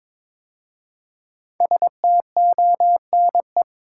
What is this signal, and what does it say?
Morse code, heard as a keyed tone, about 22 wpm, 700 Hz. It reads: STONE
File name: 5 Letter Words - First 3 Letters Given - 22wpm